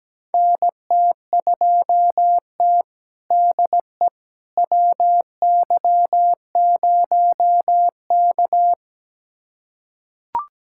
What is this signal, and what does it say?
Morse code, heard as a keyed tone, about 17 wpm, 700 Hz. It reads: NT2T DE WY0K
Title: USA Callsign DE Callsign - 4 Characters - 17wpm